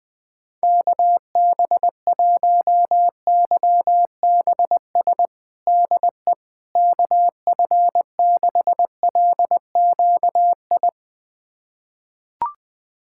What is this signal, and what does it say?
Morse code, heard as a keyed tone, about 21 wpm, 700 Hz. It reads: KB1YBS DE KF6LQI
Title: USA Callsign DE Callsign - 6 Characters - 20wpm